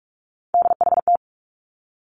Morse code, heard as a keyed tone, about 46 wpm, 700 Hz. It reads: DHT
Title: Three Challenging Letter-Number Combinations 45wpm